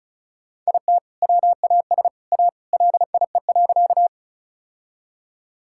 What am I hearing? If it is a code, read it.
Morse code, heard as a keyed tone, about 36 wpm, 700 Hz. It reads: IT WAS A LIE.
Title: Sentences from Top 1600 Words 35wpm